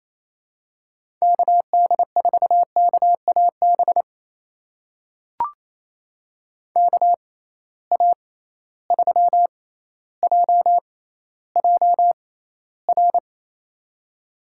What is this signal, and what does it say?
Morse code, heard as a keyed tone, about 28 wpm, spaced as for 14 wpm, 700 Hz. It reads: KD4KAB KA3JJR
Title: USA Call Signs - 6 Characters - 6x Character Spacing - 28wpm